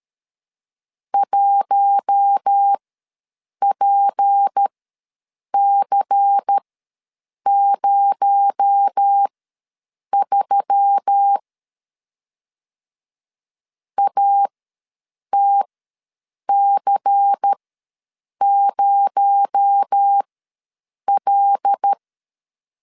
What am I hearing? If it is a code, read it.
Morse code, heard as a keyed tone, about 13 wpm, spaced as for 7 wpm, 800 Hz. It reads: 1PC03 ATC0L
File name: 7 GF9Z ws 12wpm